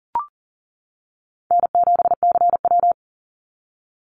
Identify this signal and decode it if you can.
Morse code, heard as a keyed tone, about 40 wpm, 700 Hz. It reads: N7CW
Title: NA SOTA Activator Callsigns 40wpm